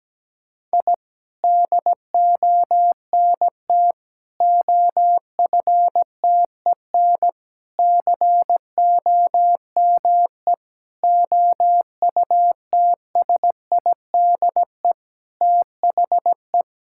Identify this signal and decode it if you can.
Morse code, heard as a keyed tone, about 17 wpm, 700 Hz. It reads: I DONT OFTEN COME OUTSIDE THE
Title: Sentences from Top 1000 Words - Review Part 6 - 17wpm